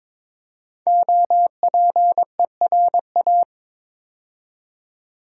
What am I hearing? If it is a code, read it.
Morse code, heard as a keyed tone, about 22 wpm, 700 Hz. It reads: OPERA
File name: 5 Letter Words - First 3 Letters Given - 22wpm